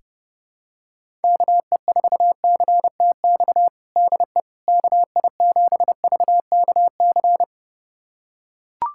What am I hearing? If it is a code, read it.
Morse code, heard as a keyed tone, about 32 wpm, 700 Hz. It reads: KE4CTX DE KI7VKC
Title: USA Callsign DE Callsign - 6 Characters - 30wpm